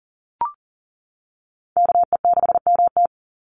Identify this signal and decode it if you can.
Morse code, heard as a keyed tone, about 38 wpm, 700 Hz. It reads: KE6MT
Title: NA SOTA Chaser Callsigns 40wpm